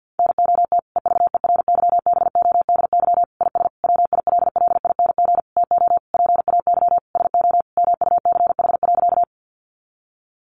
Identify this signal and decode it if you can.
Morse code, heard as a keyed tone, about 50 wpm, 700 Hz. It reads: NOT EVERYBODY IS WILLING TO PAY SO MUCH.
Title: Sentences from Top 1400 Words 50wpm